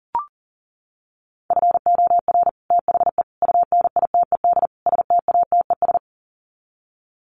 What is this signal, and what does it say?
Morse code, heard as a keyed tone, about 40 wpm, 700 Hz. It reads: FOR THE UNITED STATES
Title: Sets of 4 Words 40wpm